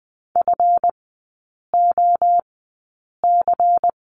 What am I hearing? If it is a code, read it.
Morse code, heard as a keyed tone, about 20 wpm, spaced as for 8 wpm, 700 Hz. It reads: FOC
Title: 166 Lesson 27e - Number 7 - Three Characters 20wpm